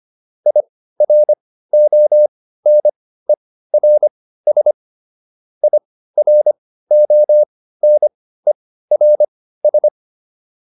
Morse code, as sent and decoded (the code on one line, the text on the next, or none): .. .-. --- -. .-. ... / .. .-. --- -. .-. ...
IRONERS IRONERS